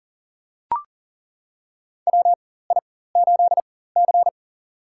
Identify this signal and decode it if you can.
Morse code, heard as a keyed tone, about 40 wpm, spaced as for 19 wpm, 700 Hz. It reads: WI8C
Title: USA Call Signs - 4 Characters - 4x Character Spacing - 40wpm